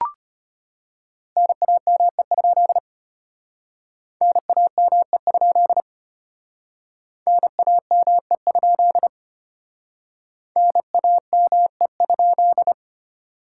Morse code, heard as a keyed tone, about 31 wpm, 700 Hz. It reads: NAME? NAME? NAME? NAME?
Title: Speed-Racing Common QSO Phrases - 25wpm